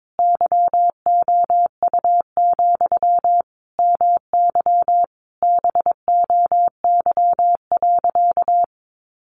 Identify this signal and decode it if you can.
Morse code, heard as a keyed tone, about 22 wpm, 700 Hz. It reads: YOU, MY BOY.
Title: Sentences from Top 700 Words 22wpm